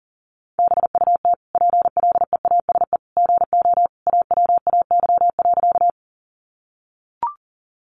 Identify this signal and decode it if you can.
Morse code, heard as a keyed tone, about 40 wpm, 700 Hz. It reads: BUT PLEASE GO AWAY.
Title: Sentences from Top 1500 Words 40wpm